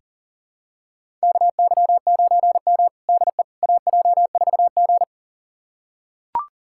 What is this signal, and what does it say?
Morse code, heard as a keyed tone, about 40 wpm, 700 Hz. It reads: KY9M DE AJ4G
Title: USA Callsign DE Callsign - 4 Characters - 40wpm